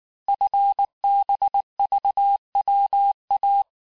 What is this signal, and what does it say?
Morse code, heard as a keyed tone, about 19 wpm, 800 Hz. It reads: FBVWA
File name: cours-du-mercredi / audios / cours-869-texte-CHIFFRE-vitesse-18-MPM